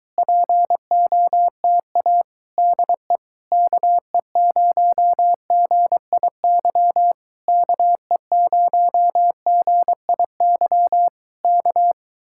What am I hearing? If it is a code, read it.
Morse code, heard as a keyed tone, about 23 wpm, 700 Hz. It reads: POTA DE KE0GIY KE0GIY K